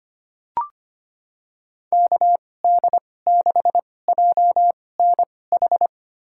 Morse code, heard as a keyed tone, about 25 wpm, spaced as for 18 wpm, 690 Hz. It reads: KD6JNH
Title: USA Call Signs - 6 Characters - 2x Character Spacing - 25wpm